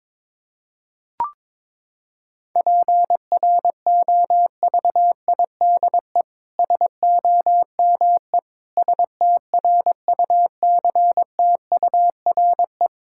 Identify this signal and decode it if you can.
Morse code, heard as a keyed tone, about 22 wpm, 700 Hz. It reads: PROVIDE SOME STRUCTURE
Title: Sentences from Top 1000 Words - Review Part 6 - 22wpm